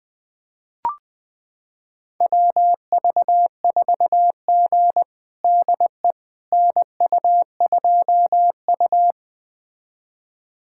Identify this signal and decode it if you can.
Morse code, heard as a keyed tone, about 20 wpm, 700 Hz. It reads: WV4G DE NU2U
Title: USA Callsign DE Callsign - 4 Characters - 20wpm